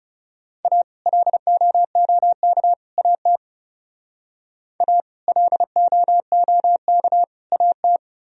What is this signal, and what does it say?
Morse code, heard as a keyed tone, about 39 wpm, 700 Hz. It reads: A LOOK AT A LOOK AT